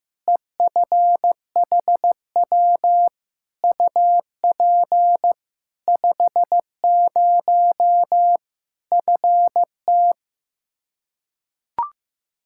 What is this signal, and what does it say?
Morse code, heard as a keyed tone, about 15 wpm, 700 Hz. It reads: EFHW UP 50 FT
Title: Practice QSO - Exchange 2 - Info - 15wpm